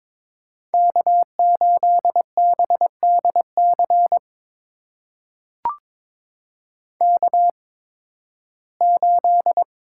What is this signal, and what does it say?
Morse code, heard as a keyed tone, about 22 wpm, spaced as for 14 wpm, 700 Hz. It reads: K8BDC K8
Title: USA Call Signs - 5 Characters - 8x Character Spacing - 22wpm